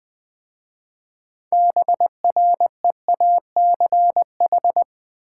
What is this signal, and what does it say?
Morse code, heard as a keyed tone, about 20 wpm, 700 Hz. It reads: BREACH
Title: 6 Letter Words - First 3 Letters Given - 20wpm